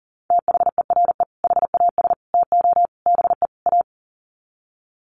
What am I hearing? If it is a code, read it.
Morse code, heard as a keyed tone, about 40 wpm, 700 Hz. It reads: THERE HAS TO BE A
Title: Sets of 5 Words Encore 40wpm